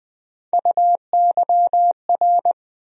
Morse code, heard as a keyed tone, about 21 wpm, 700 Hz. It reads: UYR
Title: USA Call Sign Postfixes - 3 Letters 20wpm